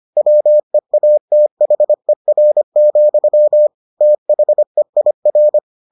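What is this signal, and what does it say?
Morse code, heard as a keyed tone, about 25 wpm, 600 Hz. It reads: WEATHER, THEIR
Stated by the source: Morse edition of the Foundations of Amateur Radio episodes.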